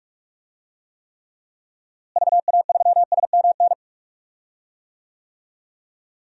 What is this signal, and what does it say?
Morse code, heard as a keyed tone, about 43 wpm, 700 Hz. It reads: VA3SMN
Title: DXCC-100 Call Signs - 6 Characters - Repeated 2x - 45wpm